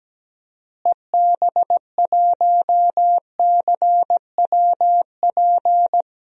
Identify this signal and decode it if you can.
Morse code, heard as a keyed tone, about 17 wpm, 700 Hz. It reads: EB1CWP
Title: DXCC-100 Call Signs - 6 Characters - Repeated 3x - 17wpm